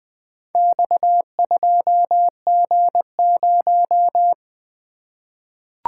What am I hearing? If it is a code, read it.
Morse code, heard as a keyed tone, about 20 wpm, 700 Hz. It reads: X2G0
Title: Four Letter-Number Combinations - 20wpm